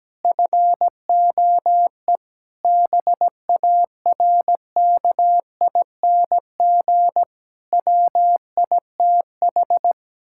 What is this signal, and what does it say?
Morse code, heard as a keyed tone, about 17 wpm, 700 Hz. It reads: FOE BARKING WITH
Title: Pangram Challenge v2 - 17wpm